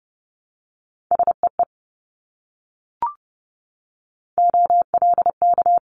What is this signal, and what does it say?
Morse code, heard as a keyed tone, about 30 wpm, 700 Hz. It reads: SEE OLK
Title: Multi-letter Phonemes Repeated 3x 30wpm